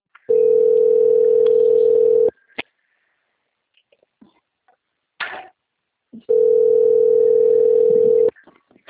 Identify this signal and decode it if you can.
Ringback tone (North America), 440 + 480 Hz.
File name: out- - - - - .75232